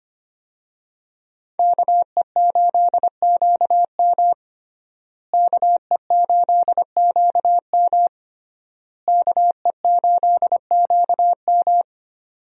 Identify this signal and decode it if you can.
Morse code, heard as a keyed tone, about 25 wpm, 700 Hz. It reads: KE8QM KE8QM KE8QM
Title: USA Call Signs - 5 Characters - Repeated 3x - 25wpm